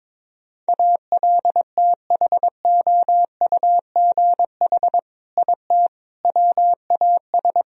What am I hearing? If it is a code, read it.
Morse code, heard as a keyed tone, about 22 wpm, 700 Hz. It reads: ALTHOUGH IT WAS
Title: Sentences from Top 1000 Words - Review Part 3 - 22wpm